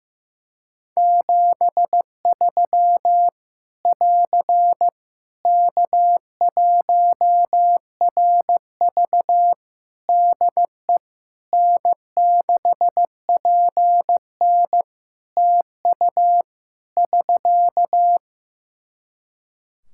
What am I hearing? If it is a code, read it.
Morse code, heard as a keyed tone, about 15 wpm, 700 Hz. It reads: 73 <AR> K1RV DE N6PN TU <SK>
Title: Practice QSO - Exchange 4 - Ending - 15wpm